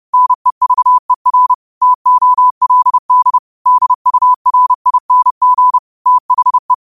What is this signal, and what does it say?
Morse code, heard as a keyed tone, about 30 wpm, 1000 Hz. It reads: NEVER TOLD DURING THE